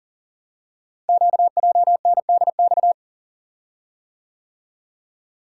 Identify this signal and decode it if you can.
Morse code, heard as a keyed tone, about 39 wpm, 700 Hz. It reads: QJNDX
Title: Code Groups - 5 Letters - 40wpm